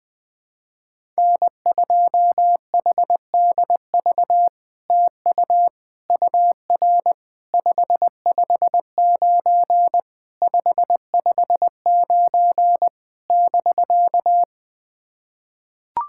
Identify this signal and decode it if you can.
Morse code, heard as a keyed tone, about 20 wpm, 700 Hz. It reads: N2HDV TU UR 559 559 <BK>
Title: SOTA QSO Practice 20wpm